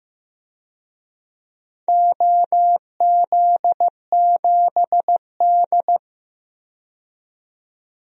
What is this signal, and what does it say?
Morse code, heard as a keyed tone, about 15 wpm, 700 Hz. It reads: OZ7D
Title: DXCC-100 Call Signs - 4 Characters - 15wpm